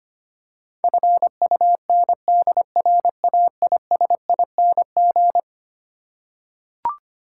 Morse code, heard as a keyed tone, about 25 wpm, 700 Hz. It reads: FUNDRAISING
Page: morse-code-ninja-practice-25wpm